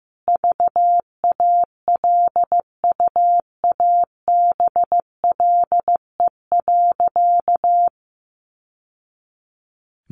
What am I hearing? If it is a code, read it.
Morse code, heard as a keyed tone, about 15 wpm, 700 Hz. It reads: VALUABLE.
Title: Sentences from Top 1100 Words 15wpm